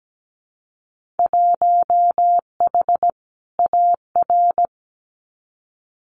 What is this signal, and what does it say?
Morse code, heard as a keyed tone, about 17 wpm, 700 Hz. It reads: 1H AR
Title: Winter Field Day Exchanges 17wpm